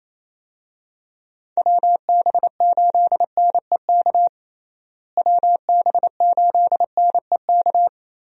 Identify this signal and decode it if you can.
Morse code, heard as a keyed tone, about 27 wpm, 700 Hz. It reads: WB8NEK WB8NEK
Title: USA Call Signs - 6 Characters - Repeated 2x - 28wpm